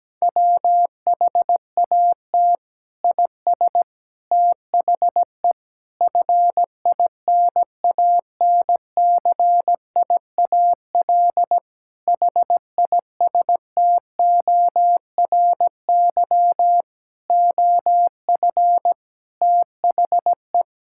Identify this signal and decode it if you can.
Morse code, heard as a keyed tone, about 17 wpm, 700 Hz. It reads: WHAT IS THE FINANCIAL HISTORY OF THE
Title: Sentences from Top 1000 Words - Review Part 5 - 17wpm